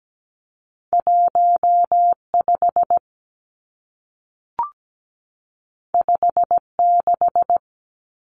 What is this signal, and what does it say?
Morse code, heard as a keyed tone, about 17 wpm, 700 Hz. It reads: 15 56
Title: Numbers - 2-Digit 17wpm